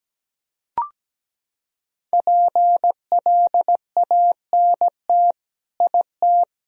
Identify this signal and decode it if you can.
Morse code, heard as a keyed tone, about 17 wpm, 700 Hz. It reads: PLANT IT
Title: Sentences from Top 1000 Words - Review Part 1 - 17wpm